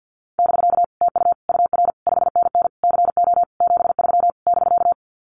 Morse code, heard as a keyed tone, about 50 wpm, 700 Hz. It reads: <BK> TU UR 5NN CO 73 <BK>
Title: Common QSO Phrases Encore 50wpm